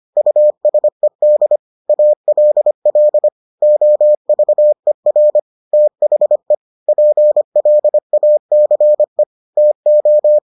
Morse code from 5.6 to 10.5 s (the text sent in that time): THE PLACE TO